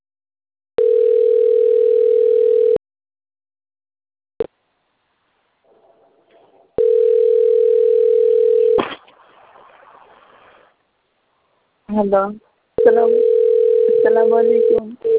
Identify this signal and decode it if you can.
Ringback tone (North America), 440 + 480 Hz, with speech.
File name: autotestagent / 10-54-33.136..callcentertransfer. .03cb003a-0691-4e2b-b524